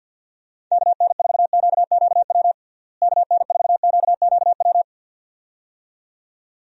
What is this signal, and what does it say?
Morse code, heard as a keyed tone, about 45 wpm, 700 Hz. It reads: KN4QQW KN4QQW
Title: USA Call Signs - 6 Characters - Repeated 2x - 50wpm